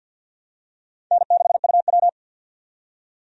Morse code, heard as a keyed tone, about 48 wpm, 700 Hz. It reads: N6UW
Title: USA Call Signs - 4 Characters - 50wpm